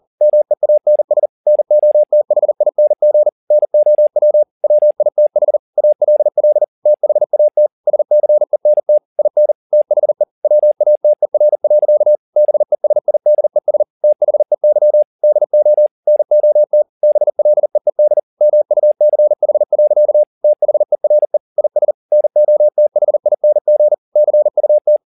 Morse code, heard as a keyed tone, about 40 wpm, 600 Hz. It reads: MEANS NOTHING NOW WITH ALL THAT SCENT IN THE WATER. BESIDES THEY DO NOT BLEED MUCH. THERE IS NOTHING CUT
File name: the old man and the sea 2 40wpm